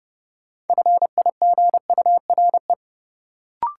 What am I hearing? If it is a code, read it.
Morse code, heard as a keyed tone, about 30 wpm, 700 Hz. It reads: FIGURE
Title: Sentences from Top 1000 Words - Review Part 2 - New Word Drill - 30wpm